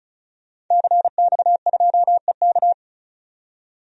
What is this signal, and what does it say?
Morse code, heard as a keyed tone, about 36 wpm, 700 Hz. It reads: CX2EK
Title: DXCC-100 Call Signs - 5 Characters - 35wpm